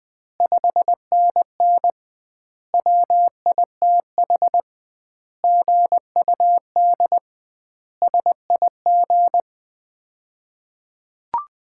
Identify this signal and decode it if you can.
Morse code, heard as a keyed tone, about 20 wpm, 700 Hz. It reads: 5NN WITH GUD SIG
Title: Common QSO Phrases - 2x Word-Spacing - 20wpm